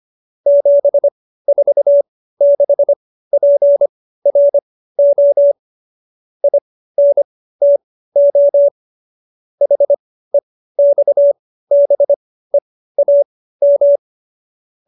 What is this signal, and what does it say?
Morse code, heard as a keyed tone, about 25 wpm, spaced as for 15 wpm, 570 Hz. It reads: 746PRO INTO HEXBEAM
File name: qso222 15